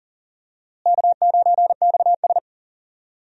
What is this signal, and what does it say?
Morse code, heard as a keyed tone, about 40 wpm, 700 Hz. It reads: K9XS
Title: USA Call Signs - 4 Characters - Repeated 3x - 40wpm